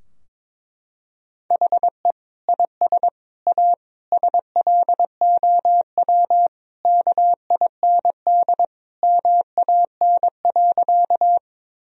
Morse code, heard as a keyed tone, about 22 wpm, 700 Hz. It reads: HE IS A SLOW KIND MAN.
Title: Sentences from Top 1000 Words - Review Part 2 - 22wpm